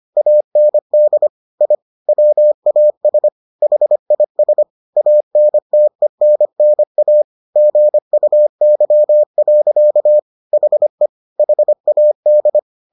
Morse code, heard as a keyed tone, about 25 wpm, 600 Hz. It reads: AND I WAS HIS ANTENNA GUY. HE HAD